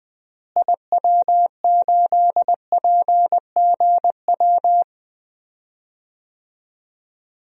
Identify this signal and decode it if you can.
Morse code, heard as a keyed tone, about 20 wpm, 700 Hz. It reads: IW8PGW
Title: DXCC-100 Call Signs - 6 Characters - Repeated 2x - 20wpm